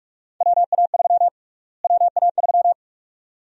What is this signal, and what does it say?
Morse code, heard as a keyed tone, about 46 wpm, 700 Hz. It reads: WA3 WA3
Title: USA Call Sign Prefixes - Repeated 2x - 45wpm